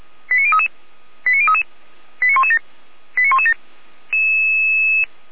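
Five-tone selective call (ZVEI), heard as quick runs of five tones, tones of about 70 ms. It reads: 89033 89033 89117 89117
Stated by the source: ZVEI five-tone selective call 89120, received by SDR.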